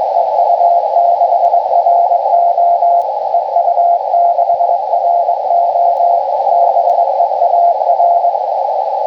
Morse code, heard as a keyed tone, about 21 wpm, 690 Hz, with radio noise. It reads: EU6RO EU6RO BEAR